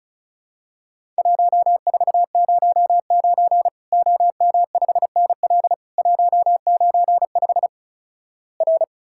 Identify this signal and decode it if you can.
Morse code, heard as a keyed tone, about 36 wpm, 700 Hz. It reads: 1409 OM5NL 195 R
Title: Worked All Europe DX Contest - QTC Exchange - 35wpm